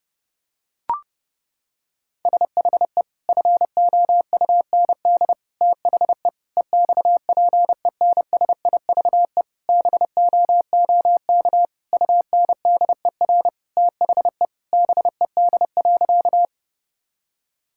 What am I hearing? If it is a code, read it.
Morse code, heard as a keyed tone, about 30 wpm, 700 Hz. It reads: SHE FOUND THE EXPENSIVE BOOK UNDER THE BED.
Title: Sentences from Top 1000 Words - Review Part 6 - 30wpm